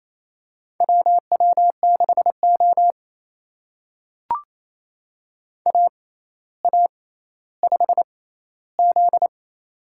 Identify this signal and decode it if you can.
Morse code, heard as a keyed tone, about 28 wpm, spaced as for 14 wpm, 700 Hz. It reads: WW6O AA5Z
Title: USA Call Signs - 4 Characters - 6x Character Spacing - 28wpm